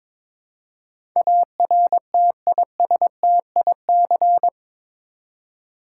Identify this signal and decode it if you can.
Morse code, heard as a keyed tone, about 22 wpm, 700 Hz. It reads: ARTISTIC
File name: practice / 22wpm / Eight Letter Words - 22wpm